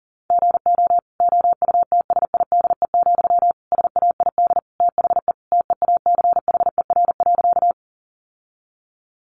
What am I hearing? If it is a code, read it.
Morse code, heard as a keyed tone, about 40 wpm, 700 Hz. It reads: GO OUTSIDE, SAID THE TEACHER.
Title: Sentences from Top 900 Words 40wpm